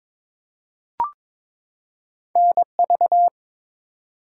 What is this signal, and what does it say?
Morse code, heard as a keyed tone, about 22 wpm, 700 Hz. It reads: NV
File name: ARRL Sections v2 - 22wpm